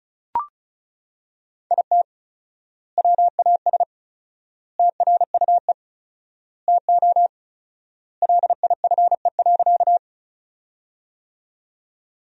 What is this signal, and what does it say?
Morse code, heard as a keyed tone, about 34 wpm, 700 Hz. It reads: IT WAS TRUE TO LIFE.
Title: 4x Word-Spacing - Sentences from Top 400 Words - 35wpm